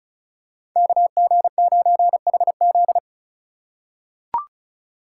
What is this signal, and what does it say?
Morse code, heard as a keyed tone, about 34 wpm, 700 Hz. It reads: KG9HZ
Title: USA Call Signs - 6x Character Spacing - 35wpm